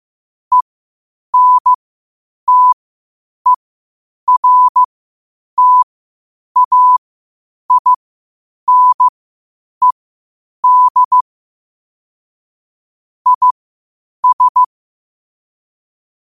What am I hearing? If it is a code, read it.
Morse code, heard as a keyed tone, about 15 wpm, spaced as for 8 wpm, 1000 Hz. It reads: ENTERTAINED IS
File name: qotdfiles / 8wpm